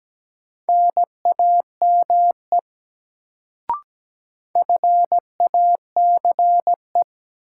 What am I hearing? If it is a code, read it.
Morse code, heard as a keyed tone, about 17 wpm, 700 Hz. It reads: NAME FACE
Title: Sets of 3 Words - New Word Drill - 17wpm